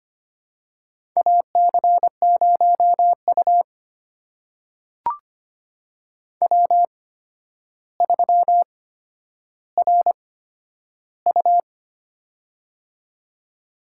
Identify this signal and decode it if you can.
Morse code, heard as a keyed tone, about 25 wpm, spaced as for 11 wpm, 700 Hz. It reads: AC0U W3RU
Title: USA Call Signs - 8x Character Spacing - 25wpm